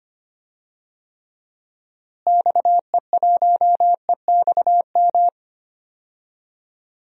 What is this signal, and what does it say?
Morse code, heard as a keyed tone, about 25 wpm, 700 Hz. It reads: XE1EXM
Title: DXCC-100 Call Signs - 6 Characters - Repeated 3x - 25wpm